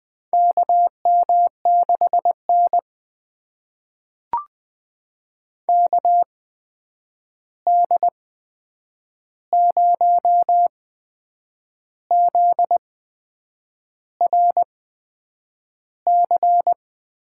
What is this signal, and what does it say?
Morse code, heard as a keyed tone, about 20 wpm, spaced as for 8 wpm, 700 Hz. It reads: KM6N KD0ZRC
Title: USA Call Signs - 8x Character Spacing - 20wpm